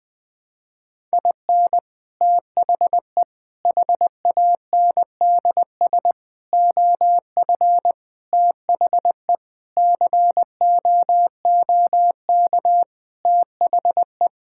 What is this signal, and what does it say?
Morse code, heard as a keyed tone, about 20 wpm, 700 Hz. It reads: IN THE HANDS OF THE COOK THE
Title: Sentences with 5-Word Phrases - 20wpm